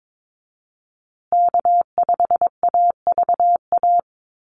Morse code, heard as a keyed tone, about 22 wpm, 700 Hz. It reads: K5AVA
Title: USA Call Signs Repeated 3x 22wpm